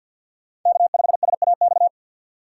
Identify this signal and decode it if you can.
Morse code, heard as a keyed tone, about 49 wpm, 700 Hz. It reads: K5SAX
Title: USA Call Signs - 5 Characters - 3x Character Spacing - 50wpm